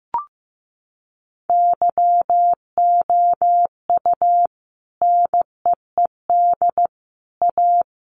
Morse code, heard as a keyed tone, about 15 wpm, 700 Hz. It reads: YOU NEED A
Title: Sentences from Top 1200 Words 15wpm